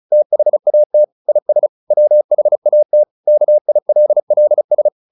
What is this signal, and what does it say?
Morse code, heard as a keyed tone, about 34 wpm, 600 Hz. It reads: THAT IS WHAT KILLS
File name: the old man and the sea / the old man and the sea 1 35wpm